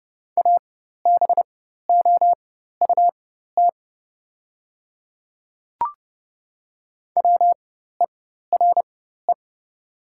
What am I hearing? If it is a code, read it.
Morse code, heard as a keyed tone, about 30 wpm, spaced as for 14 wpm, 690 Hz. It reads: ABOUT WERE
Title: 4x Character Spacing - Top 100 Words - 30wpm